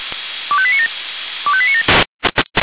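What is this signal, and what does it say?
Five-tone selective call (ZVEI), heard as quick runs of five tones, tones of about 70 ms. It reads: 25897 25897
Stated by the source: ZVEI five-tone call 25431 received on SDR.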